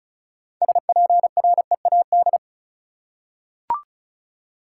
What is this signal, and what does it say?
Morse code, heard as a keyed tone, about 36 wpm, 700 Hz. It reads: SPREAD
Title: Six Letter Words - 35wpm